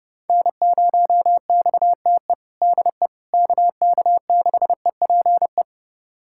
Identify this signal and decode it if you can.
Morse code, heard as a keyed tone, about 31 wpm, 700 Hz. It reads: N0XTE DE KK6EPE
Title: USA Callsign DE Callsign 30wpm